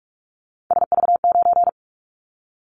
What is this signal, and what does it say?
Morse code, heard as a keyed tone, about 44 wpm, 700 Hz. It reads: SV9
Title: Callsign Prefixes for Top 100 DXCC - 45wpm